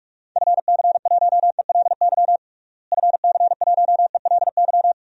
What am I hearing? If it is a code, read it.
Morse code, heard as a keyed tone, about 45 wpm, 700 Hz. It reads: FC1ELY FC1ELY
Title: DXCC-100 Call Signs - 6 Characters - Repeated 2x - 45wpm